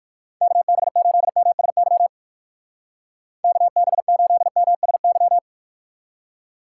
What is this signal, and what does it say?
Morse code, heard as a keyed tone, about 59 wpm, 700 Hz. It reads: KB8MSY KB8MSY